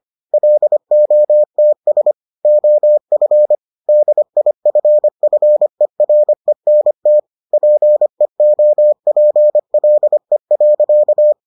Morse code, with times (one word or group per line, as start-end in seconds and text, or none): 0.3-2.1 LOTS
2.5-3.5 OF
3.9-7.2 DIFFERENT
7.5-11.3 PEOPLE.